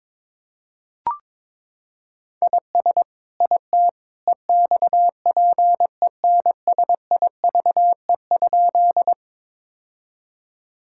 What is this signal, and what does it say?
Morse code, heard as a keyed tone, about 22 wpm, 700 Hz. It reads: IS IT EXPENSIVE?
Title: Sentences from Top 1000 Words - Review Part 6 - 22wpm